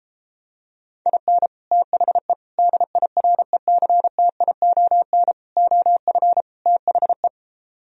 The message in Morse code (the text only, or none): IN THE DIRECTION OF THE